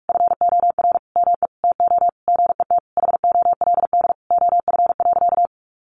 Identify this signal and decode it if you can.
Morse code, heard as a keyed tone, about 46 wpm, 700 Hz. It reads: FOR ME TO GET HOLD OF.
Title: A Sound Of Thunder 45wpm